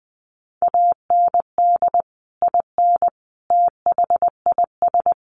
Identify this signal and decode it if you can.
Morse code, heard as a keyed tone, about 19 wpm, 700 Hz. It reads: AND IN THIS
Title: Sentences from Top 1400 Words 20wpm